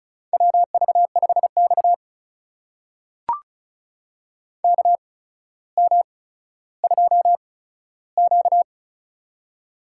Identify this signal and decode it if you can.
Morse code, heard as a keyed tone, about 36 wpm, spaced as for 15 wpm, 700 Hz. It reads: WV5X KM2Q
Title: USA Call Signs - 4 Characters - 8x Character Spacing - 35wpm